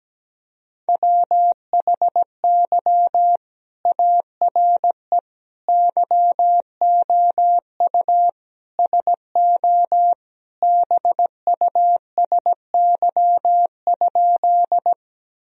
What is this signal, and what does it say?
Morse code, heard as a keyed tone, about 17 wpm, 700 Hz. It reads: WHY ARE YOU SO BUSY?
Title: Sentences from Top 1000 Words - Review Part 3 - 17wpm